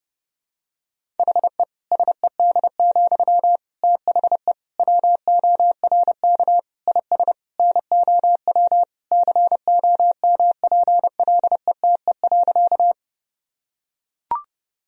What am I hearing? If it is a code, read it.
Morse code, heard as a keyed tone, about 30 wpm, 700 Hz. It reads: HE SED, THE WORK IS NOW COMPLETE.
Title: Sentences from Top 400 Words and Abbreviations 30wpm